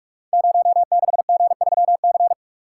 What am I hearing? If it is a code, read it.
Morse code, heard as a keyed tone, about 44 wpm, 700 Hz. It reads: 06G3C
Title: Code Groups - 5 Letter:Numbers - 45wpm